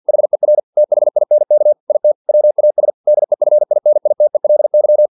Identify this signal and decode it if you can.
Morse code, heard as a keyed tone, about 50 wpm, 600 Hz. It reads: HER THINK IT WAS DEFINITELY
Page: Never 49wpm